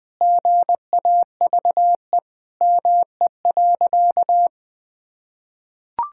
Morse code, heard as a keyed tone, about 20 wpm, 700 Hz. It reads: GAVE ME.
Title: Sentences from Top 1000 Words - Review Part 6 - 20wpm